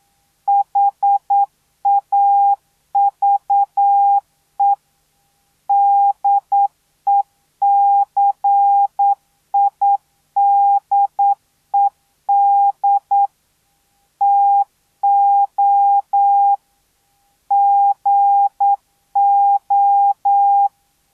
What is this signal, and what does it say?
Morse code, heard as a keyed tone, about 9 wpm, 800 Hz. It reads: HAVE DECIDED TO GO